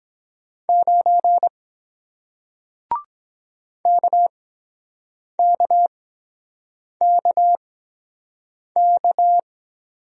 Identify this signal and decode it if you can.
Morse code, heard as a keyed tone, about 23 wpm, spaced as for 7 wpm, 700 Hz. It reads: 9 KKKK